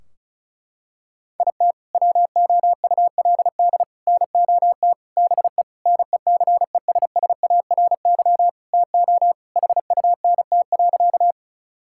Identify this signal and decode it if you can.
Morse code, heard as a keyed tone, about 34 wpm, 700 Hz. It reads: IT WOULD NOT BE NECESSARY TO HUNT.
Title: Sentences from Top 1600 Words 35wpm